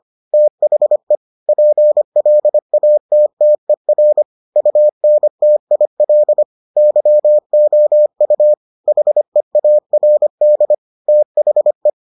Morse code, heard as a keyed tone, about 25 wpm, 600 Hz. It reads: THE PLATTER UNTIL YOU HEARD THE